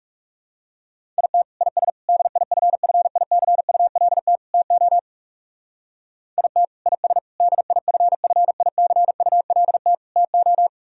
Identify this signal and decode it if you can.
Morse code, heard as a keyed tone, about 38 wpm, 700 Hz. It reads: IT IS DIFFICULT TO IT IS DIFFICULT TO